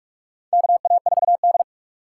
Morse code, heard as a keyed tone, about 46 wpm, 700 Hz. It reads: KA4D
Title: USA Call Signs - 4 Characters - 3x Character Spacing - 45wpm